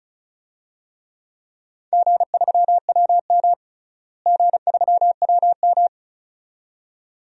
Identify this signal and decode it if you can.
Morse code, heard as a keyed tone, about 34 wpm, 700 Hz. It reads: G3WM G3WM
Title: DXCC-100 Call Signs - 4 Characters - Repeated 2x - 35wpm